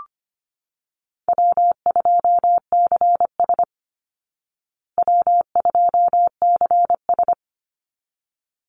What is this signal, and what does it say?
Morse code, heard as a keyed tone, about 25 wpm, 700 Hz. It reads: W2CS W2CS
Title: Easy Call Signs Repeated 2x - 25wpm